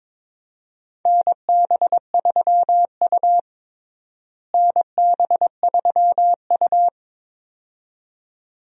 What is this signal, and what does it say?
Morse code, heard as a keyed tone, about 22 wpm, 700 Hz. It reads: NB3U NB3U
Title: USA Call Signs - 4 Characters - Repeated 2x - 22wpm